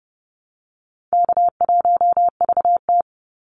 Morse code, heard as a keyed tone, about 30 wpm, 700 Hz. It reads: K1VT
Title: Easy Call Signs Repeated 3x - 30wpm